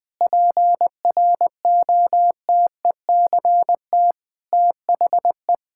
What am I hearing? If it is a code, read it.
Morse code, heard as a keyed tone, about 20 wpm, 700 Hz. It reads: PROTECT THE
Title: Difficult Sentences with Hints - 20wpm